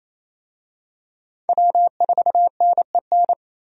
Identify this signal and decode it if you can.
Morse code, heard as a keyed tone, about 28 wpm, 700 Hz. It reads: W4NEN
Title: USA Call Signs - 5 Characters - Repeated 2x - 28wpm